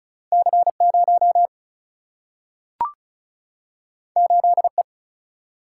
Morse code, heard as a keyed tone, about 36 wpm, 700 Hz. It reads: C 0 8 E
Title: Single-Letter Single-Number 35wpm